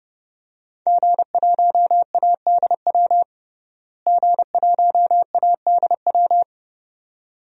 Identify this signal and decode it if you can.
Morse code, heard as a keyed tone, about 32 wpm, 700 Hz. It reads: G1ADW G1ADW
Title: DXCC-100 Call Signs - 5 Characters - Repeated 2x - 30wpm